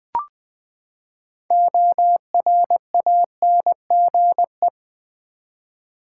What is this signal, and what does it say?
Morse code, heard as a keyed tone, about 20 wpm, 700 Hz. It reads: ORANGE
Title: Six Letter Words - 20wpm